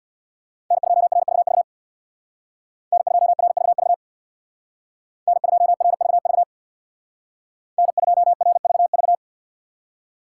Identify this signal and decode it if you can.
Morse code, heard as a keyed tone, about 64 wpm, 700 Hz. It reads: N2RVV N2RVV N2RVV N2RVV